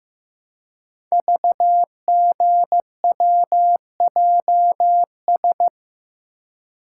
Morse code, heard as a keyed tone, about 15 wpm, 700 Hz. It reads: VGWJS
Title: Code Groups - 5 Letters - 15wpm